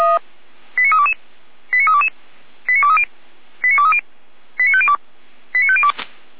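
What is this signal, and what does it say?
Five-tone selective call (ZVEI), heard as quick runs of five tones, tones of about 80 ms. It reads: 89322 89322 89230 89230 89502 89502
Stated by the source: ZVEI five-tone call 89340 received by SDR.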